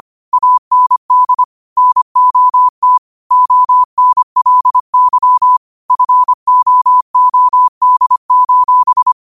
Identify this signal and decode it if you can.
Morse code, heard as a keyed tone, about 25 wpm, 1000 Hz. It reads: AND NOT ONLY FOOD:
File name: qotdfiles / 25wpm